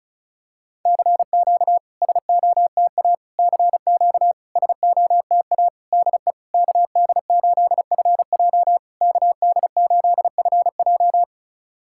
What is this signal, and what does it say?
Morse code, heard as a keyed tone, about 36 wpm, 690 Hz. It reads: CQ SOTA CQ SOTA DE KD8FJ KD8FJ
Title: SOTA QSO Practice 35wpm